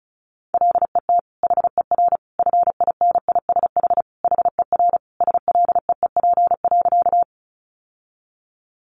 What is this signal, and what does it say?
Morse code, heard as a keyed tone, about 34 wpm, 700 Hz. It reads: LET HER FINISH HER SLEEP.
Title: Sentences from Top 1200 Words 35wpm